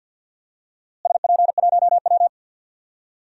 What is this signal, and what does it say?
Morse code, heard as a keyed tone, about 50 wpm, 700 Hz. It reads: SP1W
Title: DXCC-100 Call Signs - 4 Characters - Repeated 3x - 50wpm